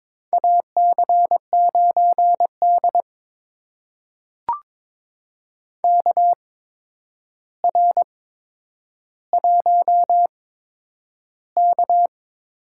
Morse code, heard as a keyed tone, about 22 wpm, spaced as for 9 wpm, 700 Hz. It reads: AC9D KR1K
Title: USA Call Signs - 4 Characters - 8x Character Spacing - 22wpm